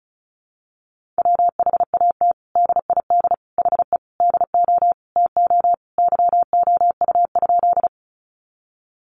Morse code, heard as a keyed tone, about 34 wpm, 700 Hz. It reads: WHAT DID HE DO TO YOU?
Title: Sentences from Top 100 Words 35wpm